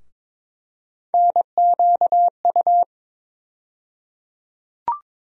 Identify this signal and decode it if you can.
Morse code, heard as a keyed tone, about 22 wpm, 700 Hz. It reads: NQU
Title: USA Call Sign Postfixes - 3 Letters 22wpm